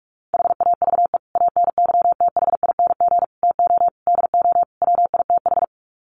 Morse code, heard as a keyed tone, about 44 wpm, 700 Hz. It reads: HAVE ANYTHING TO DO WITH
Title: Sets of 5 Words 45wpm